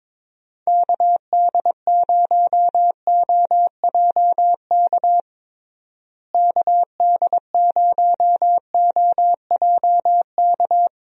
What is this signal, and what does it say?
Morse code, heard as a keyed tone, about 22 wpm, 700 Hz. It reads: KD0OJK KD0OJK
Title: USA Call Signs - 6 Characters - Repeated 3x - 22wpm